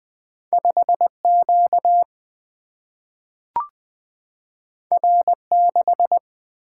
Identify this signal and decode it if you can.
Morse code, heard as a keyed tone, about 20 wpm, 700 Hz. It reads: 5 Q R 6
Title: Single-Letter Single-Number 20wpm